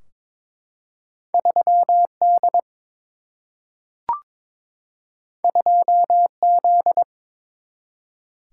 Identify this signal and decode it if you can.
Morse code, heard as a keyed tone, about 22 wpm, 700 Hz. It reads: 3D 2Z
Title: Single-Letter Single-Number 22wpm